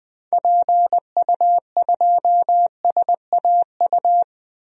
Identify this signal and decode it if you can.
Morse code, heard as a keyed tone, about 20 wpm, 700 Hz. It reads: PU2SAU
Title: DXCC-100 Call Signs - 6 Characters - Repeated 2x - 20wpm